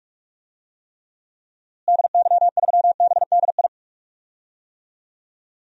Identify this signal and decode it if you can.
Morse code, heard as a keyed tone, about 44 wpm, 700 Hz. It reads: DY3BDI
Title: DXCC-100 Call Signs - 6 Characters - Repeated 3x - 45wpm